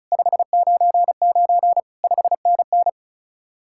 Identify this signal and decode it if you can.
Morse code, heard as a keyed tone, about 46 wpm, 700 Hz. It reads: 599 5NN